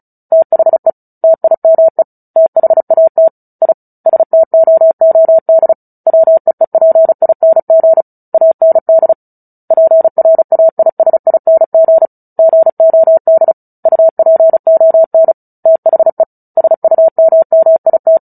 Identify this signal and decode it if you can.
Morse code, heard as a keyed tone, about 34 wpm, 650 Hz. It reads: THE TIME THAT I STOOD WEEPING AND PRAISING GOD UPON THE SUMMIT
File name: Book Two / 09-Chapter 9 - 35 WPM